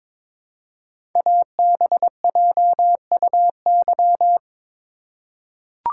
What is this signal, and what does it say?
Morse code, heard as a keyed tone, about 22 wpm, 700 Hz. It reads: ABJUY
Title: Code Groups - 5 Letters - 22wpm